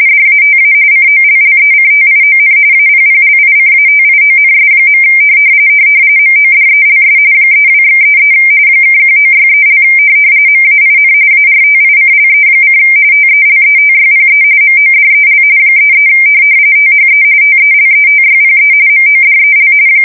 RTTY, about 45.45 baud. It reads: DATA COLLECTION APPLICATIONS. MODELS 32/33 (1963) MAIN ARTICLE: TELETYPE MODEL 33 THE TELETYPE MODEL
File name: fsk170 / teletype-corp